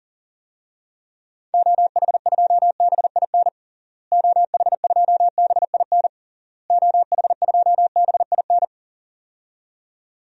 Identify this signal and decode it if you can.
Morse code, heard as a keyed tone, about 40 wpm, 700 Hz. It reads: OH2BIN OH2BIN OH2BIN
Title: DXCC-100 Call Signs - 6 Characters - Repeated 3x - 40wpm